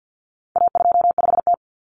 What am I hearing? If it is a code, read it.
Morse code, heard as a keyed tone, about 50 wpm, 700 Hz. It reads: A25T
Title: Common Cut Numbers - 4-Digit 50wpm